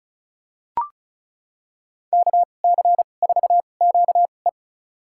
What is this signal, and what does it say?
Morse code, heard as a keyed tone, about 36 wpm, spaced as for 25 wpm, 700 Hz. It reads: KC4QE
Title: USA Call Signs - 5 Characters - 2x Character Spacing - 35wpm